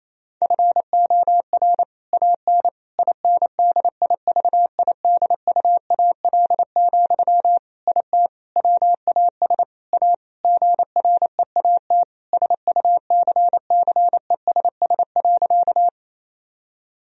Morse code, heard as a keyed tone, about 28 wpm, 700 Hz. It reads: FOR AN INDIVIDUAL, IT WAS A GREAT SUCCESS.
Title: Sentences from Top 1000 Words - Review Part 4 - 28wpm